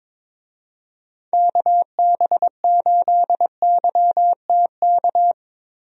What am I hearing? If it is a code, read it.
Morse code, heard as a keyed tone, about 22 wpm, 700 Hz. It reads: KB8YTK
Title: USA Call Signs - 6 Characters - Repeated 2x - 22wpm